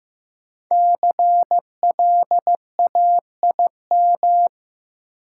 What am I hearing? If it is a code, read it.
Morse code, heard as a keyed tone, about 15 wpm, 700 Hz. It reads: CLAIM
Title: Sentences from Top 1500 Words - New Word Drill Alt - 15wpm